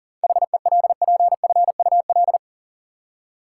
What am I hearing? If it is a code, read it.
Morse code, heard as a keyed tone, about 40 wpm, 700 Hz. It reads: HELPFUL
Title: Seven Letter Words - 40wpm